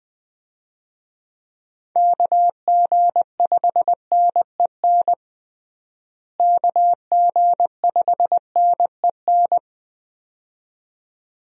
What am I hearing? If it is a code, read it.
Morse code, heard as a keyed tone, about 21 wpm, 700 Hz. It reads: KG5NEN KG5NEN
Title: USA Call Signs - 6 Characters - Repeated 2x - 20wpm